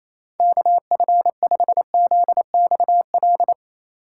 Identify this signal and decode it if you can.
Morse code, heard as a keyed tone, about 27 wpm, 700 Hz. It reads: KF5ZXL
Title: USA Call Signs - 6 Characters - 3x Character Spacing - 28wpm